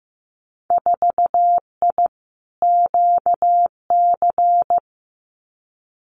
Morse code, heard as a keyed tone, about 15 wpm, 700 Hz. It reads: 4I QC
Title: Winter Field Day Exchanges Encore 15wpm